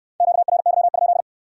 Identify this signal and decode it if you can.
Morse code, heard as a keyed tone, about 61 wpm, 700 Hz. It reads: QRY?